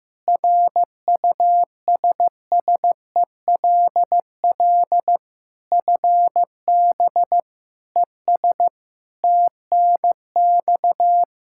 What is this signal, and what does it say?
Morse code, heard as a keyed tone, about 15 wpm, 700 Hz. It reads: RUSSELL FB ES TNX
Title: Practice QSO - Exchange 2 - Info - 15wpm